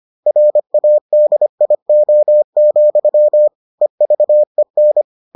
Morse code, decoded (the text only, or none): RADIO, EVEN